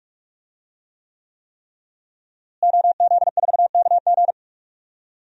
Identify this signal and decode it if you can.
Morse code, heard as a keyed tone, about 46 wpm, 700 Hz. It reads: OZ4KG
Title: DXCC-100 Call Signs - 5 Characters - Repeated 3x - 45wpm